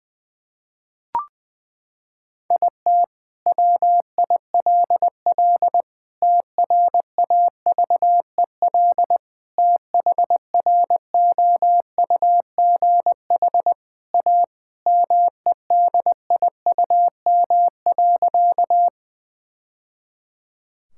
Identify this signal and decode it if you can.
Morse code, heard as a keyed tone, about 21 wpm, 700 Hz. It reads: IT WILL TRAVEL THROUGH A MEDIUM.
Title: Sentences from Top 1000 Words - Review Part 6 - 20wpm